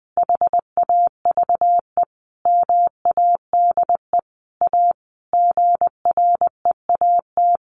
Morse code, heard as a keyed tone, about 20 wpm, 700 Hz. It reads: HAVE MADE A GREAT
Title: Sentences from Top 1500 Words 20wpm